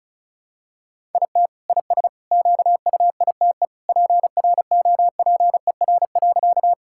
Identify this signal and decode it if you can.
Morse code, heard as a keyed tone, about 34 wpm, 690 Hz. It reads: IT IS QUITE PROPER.
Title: Sentences from Top 1000 Words - Review Part 5 - 35wpm